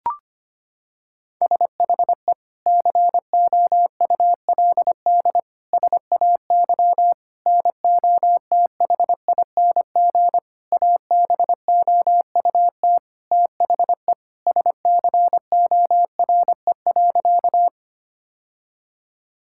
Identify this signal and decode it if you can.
Morse code, heard as a keyed tone, about 25 wpm, 700 Hz. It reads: SHE COULD SAY NOTHING ABOUT THE SCORE.
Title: Sentences from Top 1600 Words 25wpm